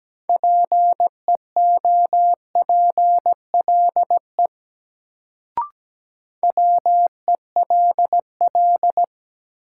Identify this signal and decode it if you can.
Morse code, heard as a keyed tone, about 17 wpm, 700 Hz. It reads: PEOPLE WELL
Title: Sets of 2 Words - New Word Drill - 17wpm